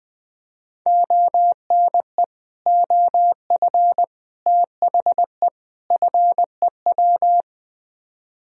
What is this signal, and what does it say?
Morse code, heard as a keyed tone, about 20 wpm, 700 Hz. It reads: ONE OF THE FEW
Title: Speed-Racing - Sets of 4 Words 17wpm